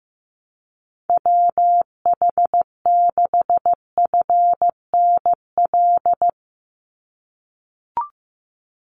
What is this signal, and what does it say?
Morse code, heard as a keyed tone, about 15 wpm, 700 Hz. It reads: WH6FNL
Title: USA Call Signs 15wpm